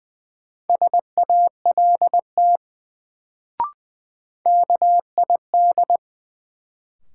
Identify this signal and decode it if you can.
Morse code, heard as a keyed tone, about 20 wpm, 700 Hz. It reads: SALT KID
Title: Sets of 5 Words - New Word Drill - 20wpm